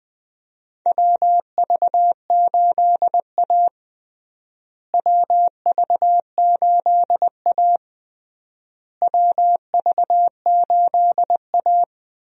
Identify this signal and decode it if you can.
Morse code, heard as a keyed tone, about 21 wpm, 700 Hz. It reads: WV8A WV8A WV8A
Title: USA Call Signs - 4 Characters - Repeated 3x - 20wpm